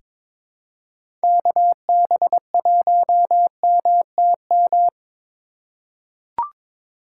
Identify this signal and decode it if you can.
Morse code, heard as a keyed tone, about 22 wpm, 700 Hz. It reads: KB1MTM
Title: USA Call Signs - 4x Character Spacing - 22wpm